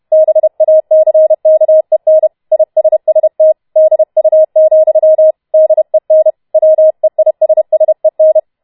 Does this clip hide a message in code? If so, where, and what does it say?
Yes: Morse code from 0.1 to 8.4 s, heard as a keyed tone, about 31 wpm, 630 Hz. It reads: BACKEN ISST DU, DEN WEISSEN